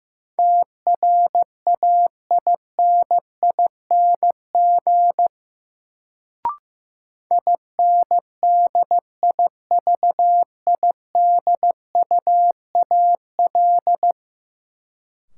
Morse code, heard as a keyed tone, about 15 wpm, 700 Hz. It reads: TRAINING INDIVIDUAL
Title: Sentences from Top 1000 Words - Review Part 4 - New Word Drill - 15wpm